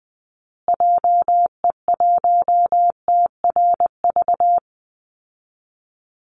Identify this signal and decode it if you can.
Morse code, heard as a keyed tone, about 20 wpm, 700 Hz. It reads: JE1TRV
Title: Call Signs Repeated 2x - 20wpm